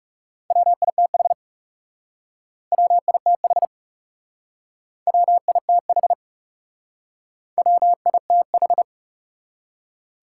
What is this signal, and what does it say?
Morse code, heard as a keyed tone, about 38 wpm, 700 Hz. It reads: WITH WITH WITH WITH